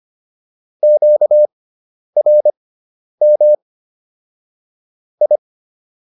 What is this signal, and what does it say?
Morse code, heard as a keyed tone, about 25 wpm, spaced as for 10 wpm, 600 Hz. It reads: QRM I